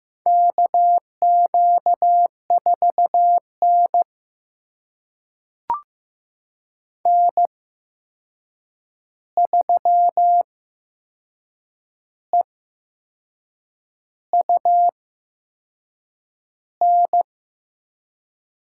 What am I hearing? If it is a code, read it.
Morse code, heard as a keyed tone, about 15 wpm, spaced as for 6 wpm, 700 Hz. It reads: KQ4N N3EUN
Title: USA Call Signs - 8x Character Spacing - 15wpm